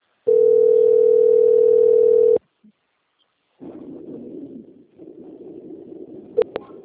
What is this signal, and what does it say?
Ringback tone (North America), 440 + 480 Hz.
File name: out- - - - - .17735